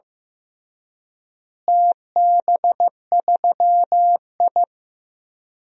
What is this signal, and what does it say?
Morse code, heard as a keyed tone, about 15 wpm, 700 Hz. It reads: TB3I
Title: DXCC-100 Call Signs - 4 Characters - Repeated 3x - 15wpm